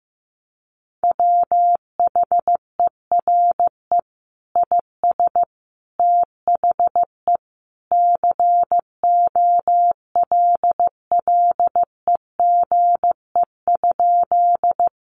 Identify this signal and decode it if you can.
Morse code, heard as a keyed tone, about 14 wpm, 700 Hz. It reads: WHERE IS THE COLLEGE?
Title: Sentences from Top 1300 Words 15wpm